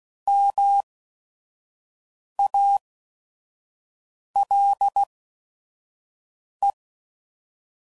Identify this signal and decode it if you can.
Morse code, heard as a keyed tone, about 16 wpm, spaced as for 5 wpm, 800 Hz. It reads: MALE